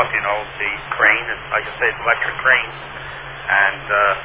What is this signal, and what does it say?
Emergency Alert System (SAME) tones, heard as harsh buzzing bursts, with speech. It is heard from 0.9 to 3.8 s.